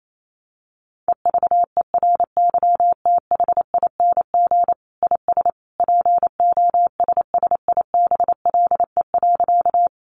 Morse code, heard as a keyed tone, about 28 wpm, 700 Hz. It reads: EVERYTHING IS POSSIBLE.
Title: Sentences from Top 700 Words 28wpm